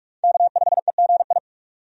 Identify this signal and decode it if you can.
Morse code, heard as a keyed tone, about 44 wpm, 700 Hz. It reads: K5EGI
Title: USA Call Signs - 5 Characters - 45wpm